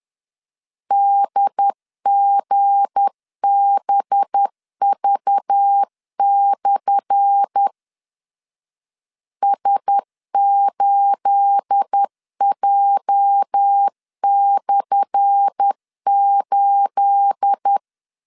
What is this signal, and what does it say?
Morse code, heard as a keyed tone, about 10 wpm, 800 Hz. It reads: DGBV/ S8J/8
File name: LEZIONI / 10 SIGNS / 10A SLASH 10wpm 2808